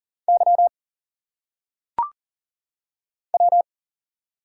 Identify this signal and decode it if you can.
Morse code, heard as a keyed tone, about 38 wpm, 700 Hz. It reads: Y W
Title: Single Letters - Phonetic Pronunciation 40wpm